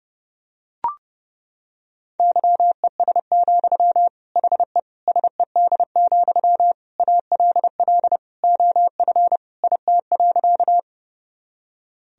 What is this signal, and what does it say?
Morse code, heard as a keyed tone, about 30 wpm, 700 Hz. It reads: YES, HE SED, ALL OF IT.
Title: Sentences from Top 500 Words and Abbreviations 30wpm